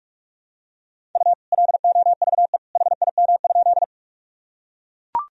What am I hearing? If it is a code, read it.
Morse code, heard as a keyed tone, about 41 wpm, 700 Hz. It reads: U LOVE HIM?
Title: Sentences from Top 400 Words and Abbreviations 45wpm